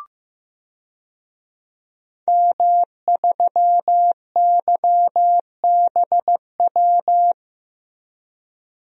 Morse code, heard as a keyed tone, about 15 wpm, 700 Hz. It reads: M3YBW
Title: DXCC-100 Call Signs - 5 Characters - Repeated 3x - 15wpm